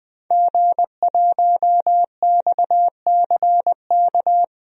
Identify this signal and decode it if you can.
Morse code, heard as a keyed tone, about 21 wpm, 700 Hz. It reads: G1XCK
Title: DXCC-100 Call Signs - 5 Characters - 20wpm